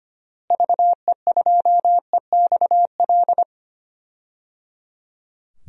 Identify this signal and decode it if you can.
Morse code, heard as a keyed tone, about 25 wpm, 700 Hz. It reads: VE2EXL
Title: DXCC-100 Call Signs - 6 Characters - Repeated 3x - 25wpm